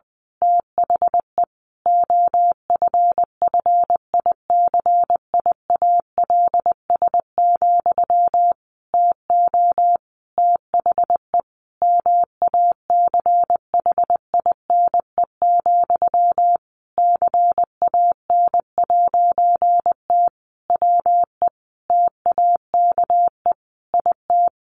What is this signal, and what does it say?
Morse code, heard as a keyed tone, about 20 wpm, 700 Hz. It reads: THE OFFICIALS, TO THE MACHINE, CAN'T WE TAKE IT
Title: A Sound Of Thunder 20wpm